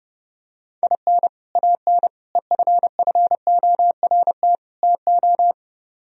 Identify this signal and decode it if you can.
Morse code, heard as a keyed tone, about 29 wpm, 700 Hz. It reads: IN AN EFFORT TO